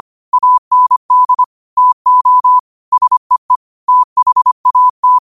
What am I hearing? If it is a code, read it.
Morse code, heard as a keyed tone, about 25 wpm, 1000 Hz. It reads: AND TO SEE THAT
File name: qotdfiles / 25wpm